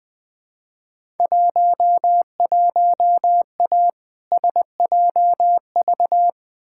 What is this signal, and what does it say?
Morse code, heard as a keyed tone, about 20 wpm, 700 Hz. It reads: 11A SJV
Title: Field Day Exchanges Encore - 20wpm